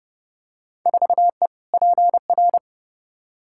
Morse code, heard as a keyed tone, about 29 wpm, 700 Hz. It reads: 4E PR
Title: Field Day Exchanges Encore - 30wpm